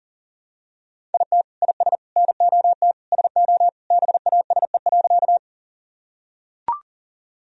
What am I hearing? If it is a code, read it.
Morse code, heard as a keyed tone, about 37 wpm, 700 Hz. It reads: IT IS NOT SO BASE.
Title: Sentences from Top 1000 Words - Review Part 1 - 40wpm